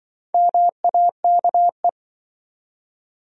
Morse code, heard as a keyed tone, about 28 wpm, 700 Hz. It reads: MAKE